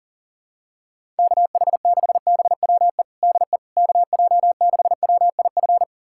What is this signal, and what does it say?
Morse code, heard as a keyed tone, about 40 wpm, 700 Hz. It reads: KH6BWE DE KJ6WIF
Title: USA Callsign DE Callsign 40wpm